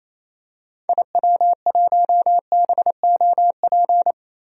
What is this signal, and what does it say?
Morse code, heard as a keyed tone, about 28 wpm, 700 Hz. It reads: IW1BOP
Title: DXCC-100 Call Signs - 6 Characters - Repeated 2x - 28wpm